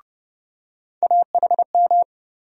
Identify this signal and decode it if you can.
Morse code, heard as a keyed tone, about 31 wpm, 700 Hz. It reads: AHM
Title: USA Call Sign Postfixes - 3 Letters 30wpm